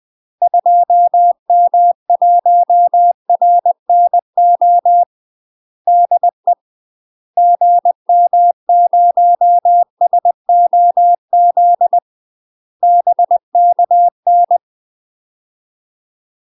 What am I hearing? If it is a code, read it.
Morse code, heard as a keyed tone, about 20 wpm, 700 Hz. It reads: 2M1RNO DE GM0SOZ BKN
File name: UK MASTER QSOs TRAINING / 20WPM / UK MASTER QSOs 0055